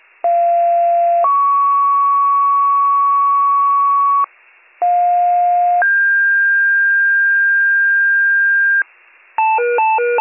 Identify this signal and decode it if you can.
Two-tone page, heard as a one-second tone, then a three-second tone, about 690 then 1120 Hz, about 710 then 1600 Hz, with noise. It is heard from 0.2 to 8.8 s.